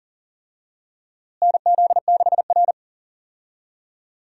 Morse code, heard as a keyed tone, about 39 wpm, 700 Hz. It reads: NZ6R
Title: USA Call Signs - 4 Characters - Repeated 2x - 40wpm